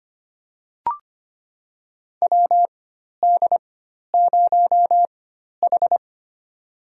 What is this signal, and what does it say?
Morse code, heard as a keyed tone, about 25 wpm, spaced as for 12 wpm, 700 Hz. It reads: WD0H
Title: USA Call Signs - 4 Characters - 4x Character Spacing - 25wpm